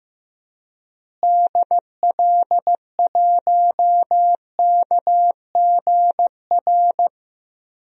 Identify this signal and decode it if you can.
Morse code, heard as a keyed tone, about 15 wpm, 700 Hz. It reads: DL1KGR
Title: DXCC-100 Call Signs - 6 Characters - Repeated 3x - 15wpm